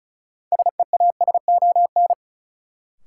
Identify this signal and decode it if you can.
Morse code, heard as a keyed tone, about 36 wpm, 700 Hz. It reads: SEASON
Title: Sets of 5 Words - New Word Drill - 35wpm